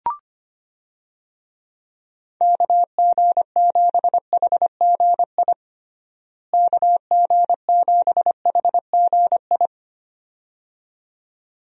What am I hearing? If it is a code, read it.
Morse code, heard as a keyed tone, about 25 wpm, 700 Hz. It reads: KG7HGI KG7HGI
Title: USA Call Signs - 6 Characters - Repeated 2x - 25wpm